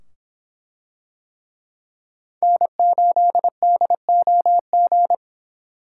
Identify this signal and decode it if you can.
Morse code, heard as a keyed tone, about 25 wpm, 700 Hz. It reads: N8DOG